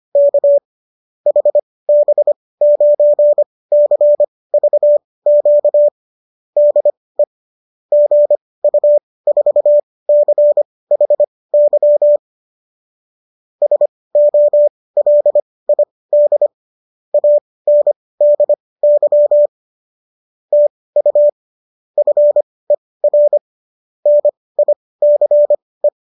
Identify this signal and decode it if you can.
Morse code, heard as a keyed tone, about 25 wpm, spaced as for 18 wpm, 590 Hz. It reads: K HB9CVQ DE GU4CHY SOLID ANDY TU FER NICE